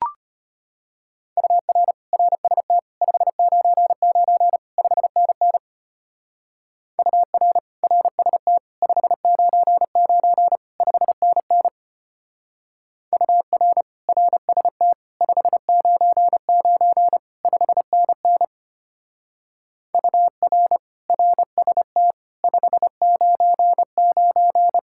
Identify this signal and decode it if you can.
Morse code, heard as a keyed tone, about 30 wpm, 700 Hz. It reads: UR RST 599 5NN UR RST 599 5NN UR RST 599 5NN UR RST 599
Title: Speed-Racing Common QSO Phrases Encore - 25wpm